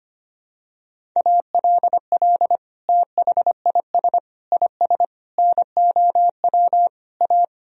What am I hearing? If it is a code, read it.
Morse code, heard as a keyed tone, about 25 wpm, 700 Hz. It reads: ALL THIS IS NOW A
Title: Sentences from Top 1000 Words - Review Part 3 - 25wpm